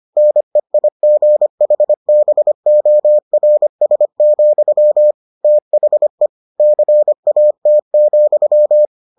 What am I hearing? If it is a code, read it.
Morse code, heard as a keyed tone, about 25 wpm, 600 Hz. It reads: NEIGHBORS, THE CAT,